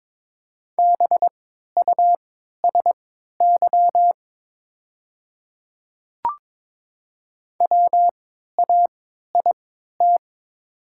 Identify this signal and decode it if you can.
Morse code, heard as a keyed tone, about 22 wpm, spaced as for 12 wpm, 700 Hz. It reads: BUSY WAIT
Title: 3x Character Spacing - Top 500 Words - 22wpm